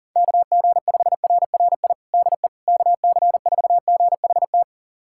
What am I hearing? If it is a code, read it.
Morse code, heard as a keyed tone, about 40 wpm, 700 Hz. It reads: KG5RRI DE KC4GHT
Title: USA Callsign DE Callsign - 6 Characters - 40wpm